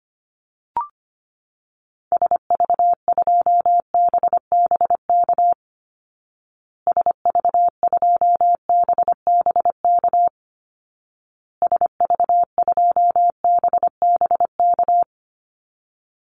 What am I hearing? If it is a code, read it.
Morse code, heard as a keyed tone, about 25 wpm, 700 Hz. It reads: SV2BBK SV2BBK SV2BBK
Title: Call Signs Repeated 2x - 25wpm